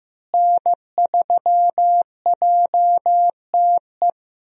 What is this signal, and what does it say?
Morse code, heard as a keyed tone, about 15 wpm, 700 Hz. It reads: N3JTE
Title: USA Call Signs - 5 Characters - Repeated 3x - 15wpm